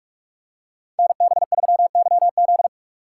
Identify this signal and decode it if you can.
Morse code, heard as a keyed tone, about 44 wpm, 700 Hz. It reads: NB3YZ
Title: Code Groups - 5 Letter:Numbers - 45wpm